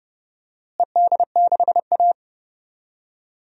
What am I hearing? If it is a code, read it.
Morse code, heard as a keyed tone, about 32 wpm, 700 Hz. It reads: ED6A
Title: DXCC-100 Call Signs - 4 Characters - Repeated 3x - 30wpm